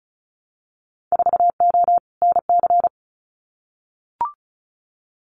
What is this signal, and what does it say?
Morse code, heard as a keyed tone, about 34 wpm, 700 Hz. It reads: 4O NC
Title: Winter Field Day Exchanges Encore 35wpm